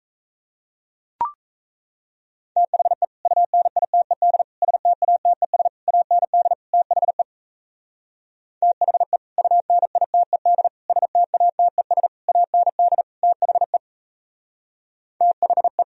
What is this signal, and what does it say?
Morse code, heard as a keyed tone, about 36 wpm, 700 Hz. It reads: THE UNITED STATES AND THE THE UNITED STATES AND THE THE